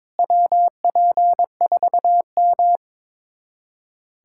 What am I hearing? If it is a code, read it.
Morse code, heard as a keyed tone, about 21 wpm, 700 Hz. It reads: WP4M
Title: USA Callsign DE Callsign - 4 Characters - 22wpm